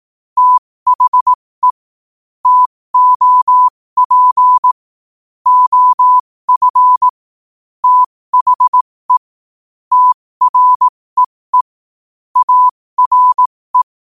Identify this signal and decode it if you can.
Morse code, heard as a keyed tone, about 18 wpm, spaced as for 15 wpm, 1000 Hz. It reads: THE TOP OF THE TREE ARE
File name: qotdfiles / 15wpm